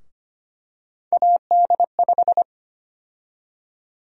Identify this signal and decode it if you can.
Morse code, heard as a keyed tone, about 25 wpm, 700 Hz. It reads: AD5
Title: USA Call Sign Prefixes - 25wpm